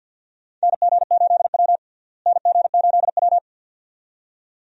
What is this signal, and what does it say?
Morse code, heard as a keyed tone, about 51 wpm, 700 Hz. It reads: NG8W NG8W
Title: USA Call Signs - 4 Characters - Repeated 2x - 50wpm